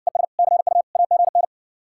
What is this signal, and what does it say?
Morse code, heard as a keyed tone, about 69 wpm, 700 Hz. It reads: ES CU AGN